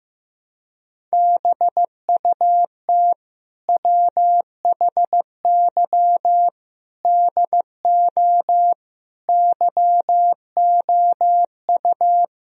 Morse code, heard as a keyed tone, about 15 wpm, 700 Hz. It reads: BUT WHY DO YOU
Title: Sentences from Top 1000 Words - Review Part 1 - 15wpm